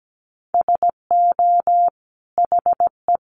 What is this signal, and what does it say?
Morse code, heard as a keyed tone, about 17 wpm, 700 Hz. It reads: SO HE
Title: Sentences from Top 1500 Words 17wpm